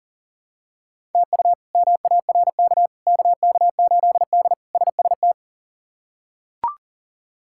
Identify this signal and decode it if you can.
Morse code, heard as a keyed tone, about 40 wpm, 700 Hz. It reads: TU MARK KK8D SST
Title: K1USN SST QSO Practice - 40wpm